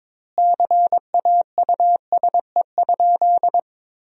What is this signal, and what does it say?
Morse code, heard as a keyed tone, about 22 wpm, 700 Hz. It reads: CAUSE?
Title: 2x Word-Spacing - Sentences from Top 200 Words - 22wpm